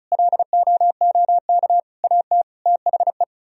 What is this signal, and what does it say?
Morse code, heard as a keyed tone, about 46 wpm, 700 Hz. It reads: LOOK AT THE